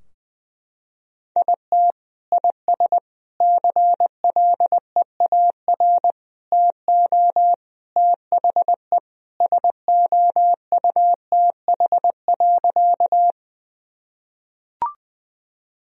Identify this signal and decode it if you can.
Morse code, heard as a keyed tone, about 20 wpm, 700 Hz. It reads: IT IS CLEAR TO THE SOUTH.
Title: Sentences from Top 1000 Words - Review Part 2 - 20wpm